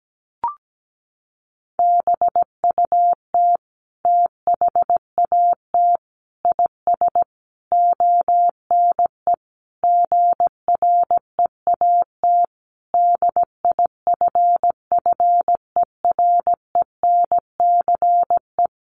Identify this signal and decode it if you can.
Morse code, heard as a keyed tone, about 17 wpm, 700 Hz. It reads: BUT THAT IS ONE GREAT DIFFERENCE
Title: Sentences from Top 800 Words 17wpm